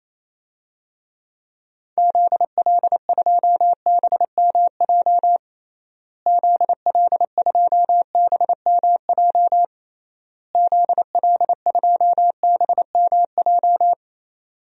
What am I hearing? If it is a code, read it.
Morse code, heard as a keyed tone, about 28 wpm, 700 Hz. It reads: ZL2BMJ ZL2BMJ ZL2BMJ
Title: DXCC-100 Call Signs - 6 Characters - Repeated 3x - 28wpm